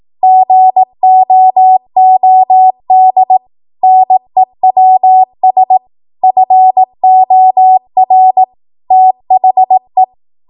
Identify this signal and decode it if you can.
Morse code, heard as a keyed tone, about 18 wpm, 750 Hz. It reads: GOOD NEWS FOR THE